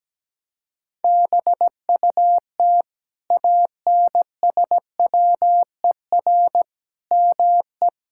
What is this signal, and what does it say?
Morse code, heard as a keyed tone, about 17 wpm, 700 Hz. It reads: BUT ANSWER ME
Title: Sentences from Top 1000 Words - Review Part 5 - 17wpm